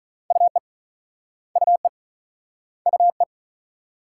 Morse code, heard as a keyed tone, about 42 wpm, 700 Hz. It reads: UE UE UE
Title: Speed-Racing Multi-letter Phonemes - 35wpm